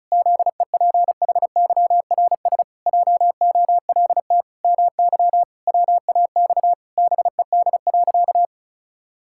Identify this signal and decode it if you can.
Morse code, heard as a keyed tone, about 34 wpm, 700 Hz. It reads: ZEPHYRS JOLT MY WAX BED.
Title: Pangram Challenge v2 - 35wpm